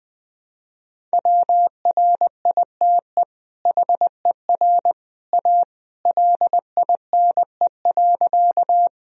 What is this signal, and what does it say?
Morse code, heard as a keyed tone, about 20 wpm, 700 Hz. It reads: WRITE HER A LINE.
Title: Sentences from Top 1000 Words - Review Part 1 - 20wpm